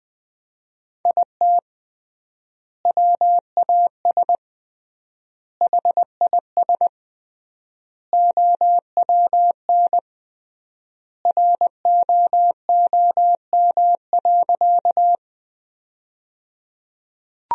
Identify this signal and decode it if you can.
Morse code, heard as a keyed tone, about 21 wpm, 700 Hz. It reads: IT WAS HIS OWN ROOM.
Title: 3x Word-Spacing - Sentences from Top 300 Words - 20wpm